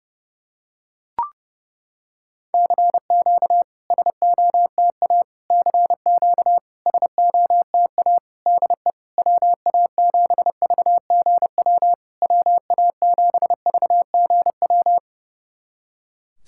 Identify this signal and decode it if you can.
Morse code, heard as a keyed tone, about 30 wpm, 690 Hz. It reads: CQ SOTA CQ SOTA DE WA7VGW WA7VGW
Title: SOTA QSO Practice 30wpm